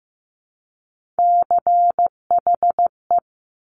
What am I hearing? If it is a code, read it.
Morse code, heard as a keyed tone, about 15 wpm, 700 Hz. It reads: CHE
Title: Top Three Letter Combinations in Words 15wpm